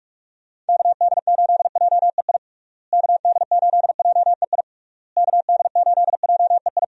Morse code, heard as a keyed tone, about 42 wpm, 700 Hz. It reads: KD8JEI KD8JEI KD8JEI
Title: USA Call Signs - 6 Characters - Repeated 3x - 45wpm